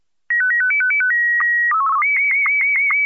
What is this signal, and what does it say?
SSTV (slow-scan television), Martin 1.
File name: sstv martin1 32bits